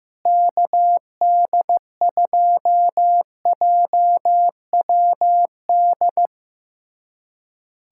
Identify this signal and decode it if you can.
Morse code, heard as a keyed tone, about 15 wpm, 700 Hz. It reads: KD2JWD
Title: USA Callsign DE Callsign 15wpm